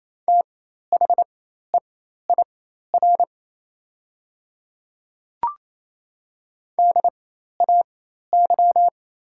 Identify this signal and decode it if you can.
Morse code, heard as a keyed tone, about 28 wpm, spaced as for 13 wpm, 700 Hz. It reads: THEIR DAY
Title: 4x Character Spacing - Top 100 Words - 28wpm